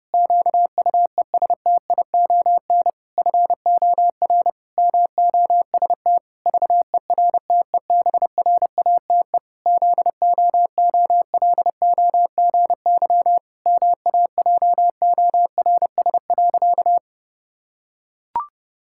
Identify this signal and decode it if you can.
Morse code, heard as a keyed tone, about 30 wpm, 700 Hz. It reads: QUESTION FOR MOST VERTEBRATE ZOOLOGY MAJORS.
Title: Pangram Challenge v2 - 30wpm